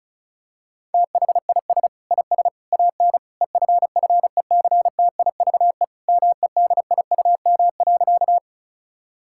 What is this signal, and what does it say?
Morse code, heard as a keyed tone, about 36 wpm, 700 Hz. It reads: THIS IS AN EFFECTIVE MEDIUM.
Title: Sentences from Top 1000 Words - Review Part 6 - 35wpm